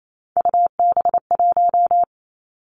Morse code, heard as a keyed tone, about 28 wpm, 700 Hz. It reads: UB1
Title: Three Challenging Letter-Number Combinations 28wpm